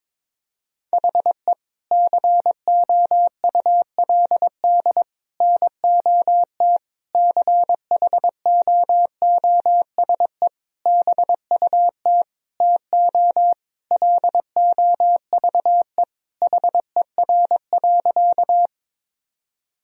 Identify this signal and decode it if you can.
Morse code, heard as a keyed tone, about 22 wpm, 700 Hz. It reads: HE COULD NOT CHOOSE BUT TO LOVE HER.
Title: Sentences from Top 1000 Words - Review Part 4 - 22wpm